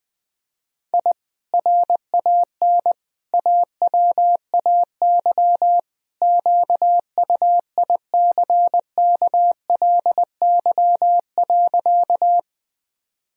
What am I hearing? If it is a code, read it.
Morse code, heard as a keyed tone, about 20 wpm, 700 Hz. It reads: I RAN AWAY QUICKLY.
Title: Sentences from Top 1000 Words - Review Part 3 - 20wpm